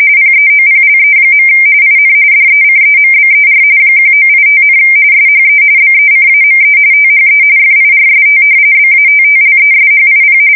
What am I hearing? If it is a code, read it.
RTTY, about 45.45 baud. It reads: PEOPLE. IN DECEMBER 1985, SHE BECAME THE FIRST BATTLESHIP
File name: fsk170 / battleshipnj